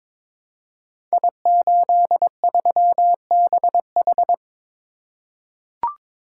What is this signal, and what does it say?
Morse code, heard as a keyed tone, about 22 wpm, 700 Hz. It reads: I83BH
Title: Code Groups - 5 Letter:Numbers - 22wpm